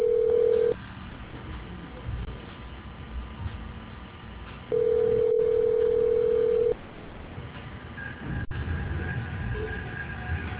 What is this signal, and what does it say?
Ringback tone (North America), 440 + 480 Hz.